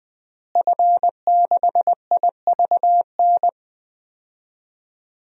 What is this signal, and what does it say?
Morse code, heard as a keyed tone, about 20 wpm, 700 Hz. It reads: F6IVN
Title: DXCC-100 Call Signs - 5 Characters - Repeated 3x - 20wpm